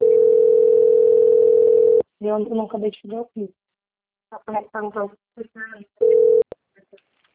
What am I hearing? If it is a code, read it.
Ringback tone (North America), 440 + 480 Hz, with speech.
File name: out- - - - - .35046